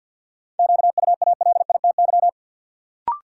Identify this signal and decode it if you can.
Morse code, heard as a keyed tone, about 52 wpm, 700 Hz. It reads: QUALITY
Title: Sentences from Top 1000 Words - Review Part 4 - New Word Drill - 50wpm